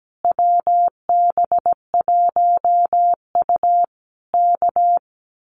Morse code, heard as a keyed tone, about 17 wpm, 700 Hz. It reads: WB1U K
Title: Common QSO Phrases Encore 17wpm